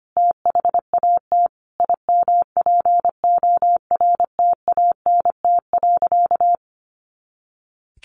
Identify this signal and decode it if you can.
Morse code, heard as a keyed tone, about 25 wpm, 700 Hz. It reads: THAT IMPORTANT.
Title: A Sound Of Thunder 25wpm